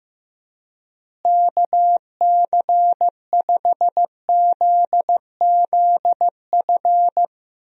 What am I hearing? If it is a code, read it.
Morse code, heard as a keyed tone, about 16 wpm, 700 Hz. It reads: KC5ZZF
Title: USA Call Signs - 6 Characters - Repeated 3x - 15wpm